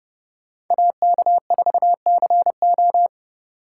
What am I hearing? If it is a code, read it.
Morse code, heard as a keyed tone, about 29 wpm, 700 Hz. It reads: AK4CO